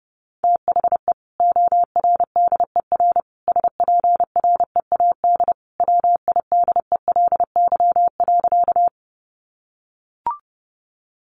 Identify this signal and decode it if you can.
Morse code, heard as a keyed tone, about 30 wpm, 700 Hz. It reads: THE ORDER SPREAD WIDELY.
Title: Sentences from Top 1200 Words 30wpm